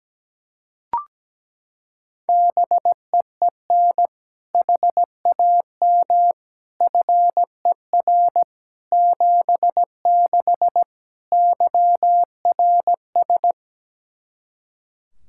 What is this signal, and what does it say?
Morse code, heard as a keyed tone, about 17 wpm, 700 Hz. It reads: BEEN HAM FER 76 YRS
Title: K1USN SST QSO Practice 17wpm